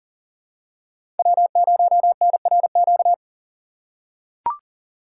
Morse code, heard as a keyed tone, about 40 wpm, 700 Hz. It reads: W0NRQ
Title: USA Call Signs - 5 Characters - 3x Character Spacing - 40wpm